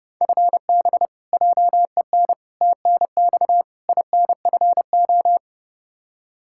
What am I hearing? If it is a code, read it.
Morse code, heard as a keyed tone, about 30 wpm, 700 Hz. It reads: FB JEN TNX INFO
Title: Practice QSO - Exchange 3 - More Info - 30wpm